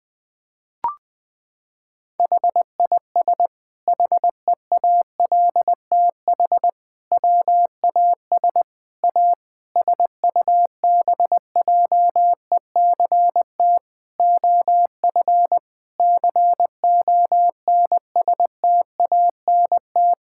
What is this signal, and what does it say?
Morse code, heard as a keyed tone, about 20 wpm, 700 Hz. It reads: HIS HEALTH WAS A SUBJECT OF CONSTANT
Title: Sentences from Top 1600 Words 20wpm